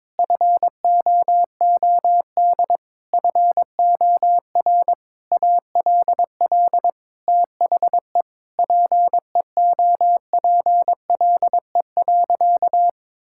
Morse code, heard as a keyed tone, about 22 wpm, 700 Hz. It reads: FOOD FOR ALL THE PEOPLE.
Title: Sentences from Top 1600 Words 22wpm